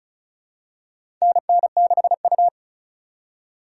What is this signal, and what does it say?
Morse code, heard as a keyed tone, about 36 wpm, 700 Hz. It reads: NN6U
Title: USA Call Signs - 4 Characters - 3x Character Spacing - 35wpm